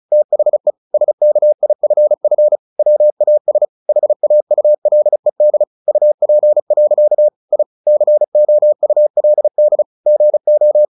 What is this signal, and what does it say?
Morse code, heard as a keyed tone, about 35 wpm, 600 Hz. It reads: THE SKIFF WAS HAULED UP. I COULD GO
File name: the old man and the sea / the old man and the sea 1 35wpm